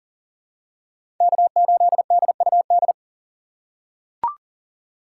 Morse code, heard as a keyed tone, about 40 wpm, 700 Hz. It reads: K8DUD
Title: USA Call Signs - 5 Characters - 6x Character Spacing - 40wpm